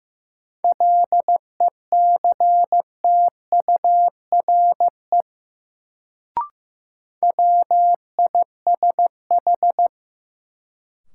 Morse code, heard as a keyed tone, about 15 wpm, 700 Hz. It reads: LECTURE WISH
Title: Sentences from Top 1600 Words - New Word Drill Alt - 15wpm